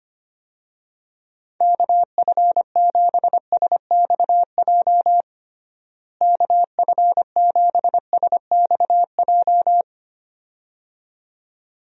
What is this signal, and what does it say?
Morse code, heard as a keyed tone, about 25 wpm, 700 Hz. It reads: KF7SXJ KF7SXJ
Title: USA Call Signs - 6 Characters - Repeated 2x - 25wpm